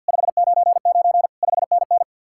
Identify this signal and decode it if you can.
Morse code, heard as a keyed tone, about 66 wpm, spaced as for 58 wpm, 700 Hz. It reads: 599 5NN
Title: Speed-Racing Common QSO Phrases - 50wpm